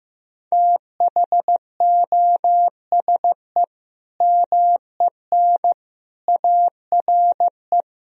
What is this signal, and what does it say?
Morse code, heard as a keyed tone, about 15 wpm, 700 Hz. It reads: THOSE MEN ARE
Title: Sentences from Top 1000 Words - Review Part 1 - 15wpm